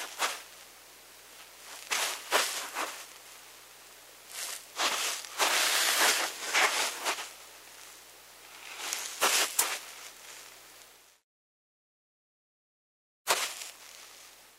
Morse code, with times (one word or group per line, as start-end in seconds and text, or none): none